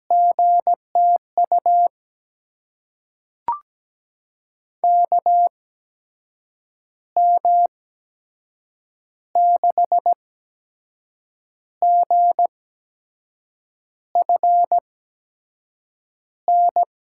Morse code, heard as a keyed tone, about 17 wpm, spaced as for 6 wpm, 700 Hz. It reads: GTU KM6GFN
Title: USA Call Signs - 6 Characters - 8x Character Spacing - 17wpm